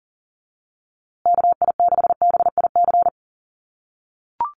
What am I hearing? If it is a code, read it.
Morse code, heard as a keyed tone, about 40 wpm, 700 Hz. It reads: KI6BIC
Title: USA Call Signs Repeated 3x 40wpm